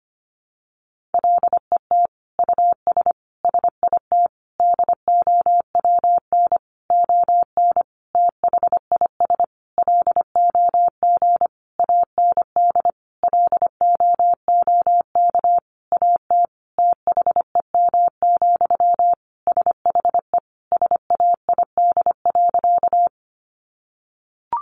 Morse code, heard as a keyed tone, about 25 wpm, 700 Hz. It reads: LET US SIT DOWN ON THIS LOG AND LOOK AT THEM, SHE SAID.
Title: Sentences from Top 1400 Words 25wpm